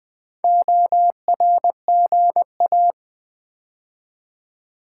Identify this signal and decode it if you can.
Morse code, heard as a keyed tone, about 20 wpm, 700 Hz. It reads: ORGA
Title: Build A Word - 4 Letters At A Time - 12-Letter Words - 20wpm